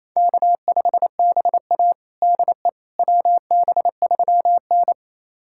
Morse code, heard as a keyed tone, about 28 wpm, 700 Hz. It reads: K5BA DE WB3N
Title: USA Callsign DE Callsign - 4 Characters - 28wpm